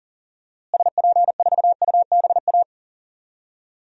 Morse code, heard as a keyed tone, about 40 wpm, 700 Hz. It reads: SP4UBA
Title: DXCC-100 Call Signs - 6 Characters - Repeated 2x - 40wpm